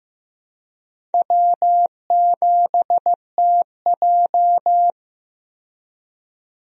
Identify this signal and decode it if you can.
Morse code, heard as a keyed tone, about 15 wpm, 700 Hz. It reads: W7TJ
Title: USA Call Signs - 4 Characters - 15wpm